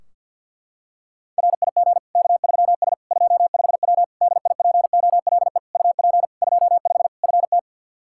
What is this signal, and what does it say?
Morse code, heard as a keyed tone, about 50 wpm, 700 Hz. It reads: RIG K3S 25W DIPOLE UP 25 FT
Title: K1USN SST QSO Practice 50wpm